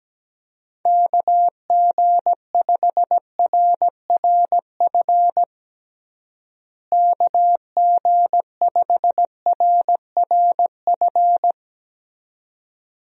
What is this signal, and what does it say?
Morse code, heard as a keyed tone, about 17 wpm, 700 Hz. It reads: KG5RRF KG5RRF
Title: USA Call Signs - 6 Characters - Repeated 2x - 17wpm